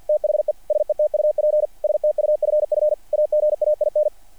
Morse code, heard as a keyed tone, about 48 wpm, 600 Hz. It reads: THE SETUP STUFF AGAIN
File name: archived / CRSnet